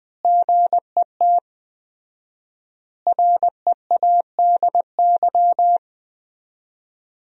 Morse code, heard as a keyed tone, about 20 wpm, 700 Hz. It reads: GET READY
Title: 4x Word-Spacing - Sentences from Top 400 Words - 20wpm